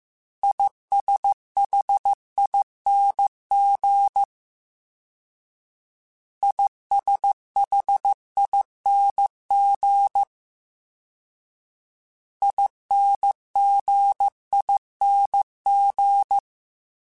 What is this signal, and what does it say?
Morse code, heard as a keyed tone, about 15 wpm, 800 Hz. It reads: ISHING ISHING INGING
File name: Combinations / Combinations-mp3s / Combinations@13